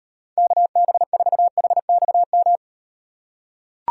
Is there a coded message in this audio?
Yes: Morse code from 0.4 to 2.6 s, heard as a keyed tone, about 38 wpm, 700 Hz. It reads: KB4HXM